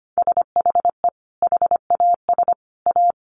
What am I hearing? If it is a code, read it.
Morse code, heard as a keyed tone, about 25 wpm, 700 Hz. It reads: SHE HAS A
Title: Sentences from Top 800 Words 25wpm